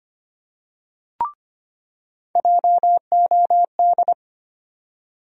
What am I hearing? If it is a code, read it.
Morse code, heard as a keyed tone, about 25 wpm, 700 Hz. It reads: JOD
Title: USA Call Sign Postfixes - 3 Letters 25wpm